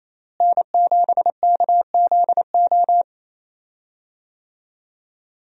Morse code, heard as a keyed tone, about 28 wpm, 700 Hz. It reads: N7KZO
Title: USA Call Signs - 5 Characters - Repeated 3x - 28wpm